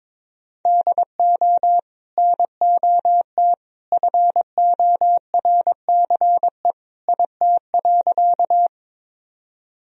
Morse code, heard as a keyed tone, about 22 wpm, 700 Hz. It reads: DO NOT FORCE IT.
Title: Sentences from Top 1000 Words - Review Part 3 - 22wpm